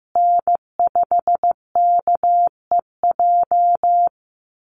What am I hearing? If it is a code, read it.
Morse code, heard as a keyed tone, about 15 wpm, 700 Hz. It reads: N5KEJ
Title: USA Call Signs Repeated 2x 15wpm